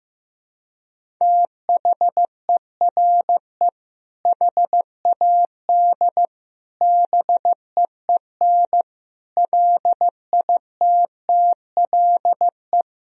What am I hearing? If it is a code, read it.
Morse code, heard as a keyed tone, about 15 wpm, 700 Hz. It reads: THERE HAD BEEN LITTLE
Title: Sentences from Top 1000 Words - Review Part 1 - 15wpm